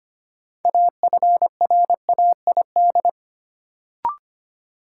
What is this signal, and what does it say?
Morse code, heard as a keyed tone, about 25 wpm, 700 Hz. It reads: AFRAID
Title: Sentences from Top 1500 Words - New Word Drill - 25wpm